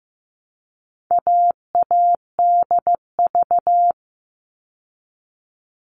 Morse code, heard as a keyed tone, about 15 wpm, 700 Hz. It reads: AADV
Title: All Cut Numbers - 4-Digit 15wpm